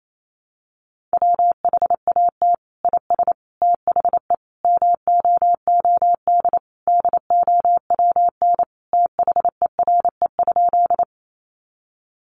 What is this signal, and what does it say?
Morse code, heard as a keyed tone, about 28 wpm, 700 Hz. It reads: WHAT IS THE MOOD DOWN THERE?
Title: Sentences from Top 1100 Words 28wpm